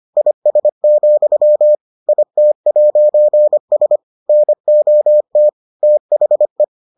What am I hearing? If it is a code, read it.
Morse code, heard as a keyed tone, about 25 wpm, 600 Hz. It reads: IS, IT'S NOT THE